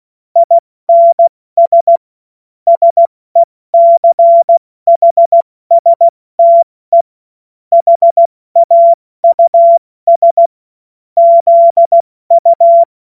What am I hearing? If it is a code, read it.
Morse code, heard as a keyed tone, about 16 wpm, 680 Hz. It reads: INS SECHSTE HAUS ZU
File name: Morsekurs / Witze / Witz 15. August 2023 70 F80